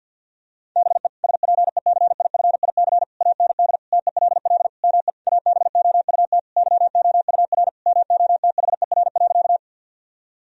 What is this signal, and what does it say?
Morse code, heard as a keyed tone, about 50 wpm, 700 Hz. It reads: BE SPECIFIC AND TELL ME ABOUT YOUR MOTHER.
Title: Sentences from Top 1000 Words - Review Part 3 - 50wpm